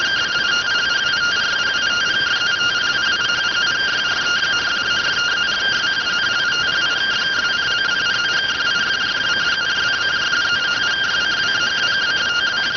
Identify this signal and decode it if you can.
Two-tone teleprinter signal, about 170 Hz shift.